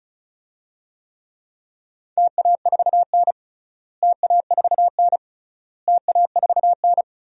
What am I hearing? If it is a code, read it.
Morse code, heard as a keyed tone, about 36 wpm, 700 Hz. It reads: TA4N TA4N TA4N
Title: DXCC-100 Call Signs - 4 Characters - Repeated 3x - 35wpm